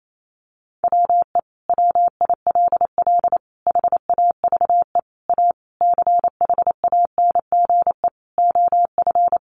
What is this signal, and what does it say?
Morse code, heard as a keyed tone, about 28 wpm, 700 Hz. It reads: WE WILL HAVE A CHANGE OF
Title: Sentences from Top 1400 Words 28wpm